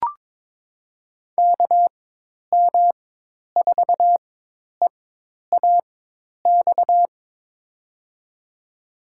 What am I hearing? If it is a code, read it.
Morse code, heard as a keyed tone, about 22 wpm, spaced as for 10 wpm, 700 Hz. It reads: KM4EAX
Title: USA Call Signs - 6 Characters - 4x Character Spacing - 22wpm